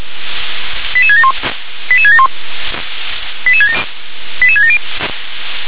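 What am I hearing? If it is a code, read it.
Five-tone selective call (ZVEI), heard as quick runs of five tones, tones of about 70 ms. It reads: 88671 88671 88670
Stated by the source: ZVEI five-tone selective call received with an SDR.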